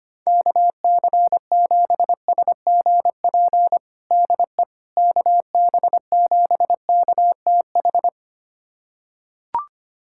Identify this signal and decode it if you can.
Morse code, heard as a keyed tone, about 25 wpm, 700 Hz. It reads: KC7SGP DE KB7KTH
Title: USA Callsign DE Callsign 25wpm